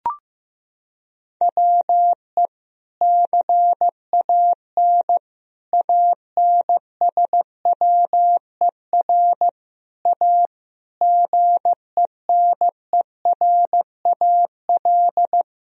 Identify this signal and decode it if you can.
Morse code, heard as a keyed tone, about 16 wpm, 700 Hz. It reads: WE CAN ANSWER A GENERAL
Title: Sentences from Top 1000 Words - Review Part 3 - 15wpm